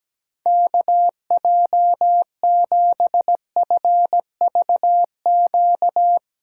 Morse code, heard as a keyed tone, about 17 wpm, 700 Hz. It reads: KJ7FVQ
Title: USA Call Signs - 3x Character Spacing - 17wpm